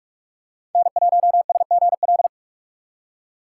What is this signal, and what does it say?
Morse code, heard as a keyed tone, about 46 wpm, 700 Hz. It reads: N1SGL
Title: USA Call Signs - 2x Character Spacing - 45wpm